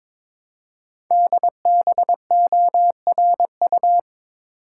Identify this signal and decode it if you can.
Morse code, heard as a keyed tone, about 22 wpm, 700 Hz. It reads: DBORU
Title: Code Groups - 5 Letters - 22wpm